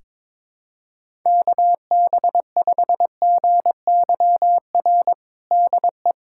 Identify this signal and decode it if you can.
Morse code, heard as a keyed tone, about 22 wpm, 700 Hz. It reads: KB5GYR DE
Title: USA Callsign DE Callsign 22wpm